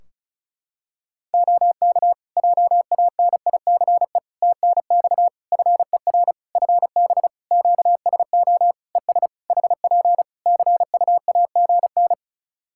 Morse code, heard as a keyed tone, about 36 wpm, 700 Hz. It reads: OK JANICE TNX FER FB QSO ES HP CUAGN
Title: K1USN SST QSO Practice 35wpm